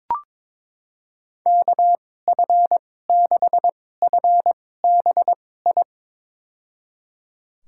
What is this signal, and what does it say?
Morse code, heard as a keyed tone, about 23 wpm, spaced as for 16 wpm, 700 Hz. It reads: KF6FBI
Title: USA Call Signs - 6 Characters - 2x Character Spacing - 22wpm